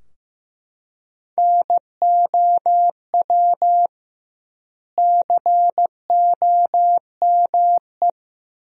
Morse code, heard as a keyed tone, about 15 wpm, 700 Hz. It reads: NOW COME
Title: 2x Word-Spacing - Sentences from Top 200 Words - 15wpm